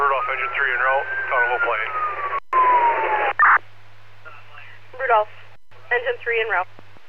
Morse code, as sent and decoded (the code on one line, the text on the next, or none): none